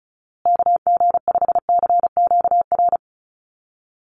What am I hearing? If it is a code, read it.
Morse code, heard as a keyed tone, about 34 wpm, 700 Hz. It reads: KG5CQR
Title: USA Call Signs Repeated 3x 35wpm